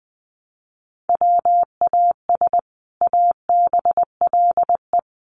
Morse code, heard as a keyed tone, about 20 wpm, 700 Hz. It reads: WAS ABLE
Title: Sets of 3 Words 20wpm